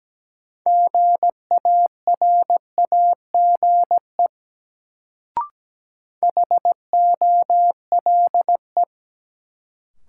Morse code, heard as a keyed tone, about 17 wpm, 700 Hz. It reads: GARAGE HOLE
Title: Sentences from Top 1600 Words - New Word Drill - 17wpm